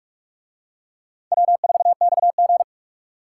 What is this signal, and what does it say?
Morse code, heard as a keyed tone, about 44 wpm, 700 Hz. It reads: W4XG